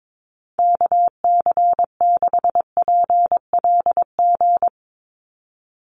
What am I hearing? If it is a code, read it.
Morse code, heard as a keyed tone, about 22 wpm, 700 Hz. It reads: KC6PLG
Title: USA Call Signs 22wpm